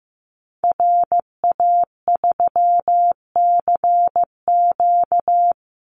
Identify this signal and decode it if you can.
Morse code, heard as a keyed tone, about 15 wpm, 700 Hz. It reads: RA3CQ
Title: Call Signs Repeated 3x - 15wpm